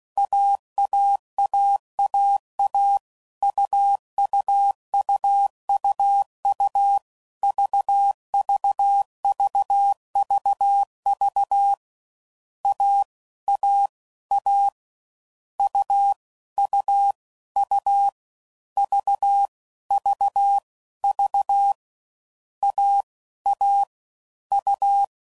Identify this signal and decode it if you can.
Morse code, heard as a keyed tone, about 16 wpm, 800 Hz. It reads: AAAAA UUUUU VVVVV A A A U U U V V V A A U
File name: Combinations / Difficult Combos / AUV AUV4